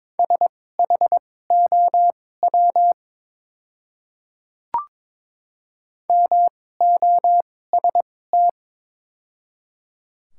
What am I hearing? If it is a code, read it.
Morse code, heard as a keyed tone, about 22 wpm, spaced as for 16 wpm, 700 Hz. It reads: SHOW MOST
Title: 2x Character Spacing - Top 300 Words - 22wpm